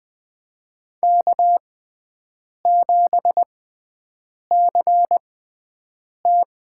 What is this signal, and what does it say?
Morse code, heard as a keyed tone, about 20 wpm, spaced as for 7 wpm, 700 Hz. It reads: K7CT
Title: USA Call Signs - 6x Character Spacing - 20wpm